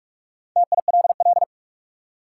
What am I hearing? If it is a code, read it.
Morse code, heard as a keyed tone, about 46 wpm, 700 Hz. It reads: TILL
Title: Four Letter Words - 45wpm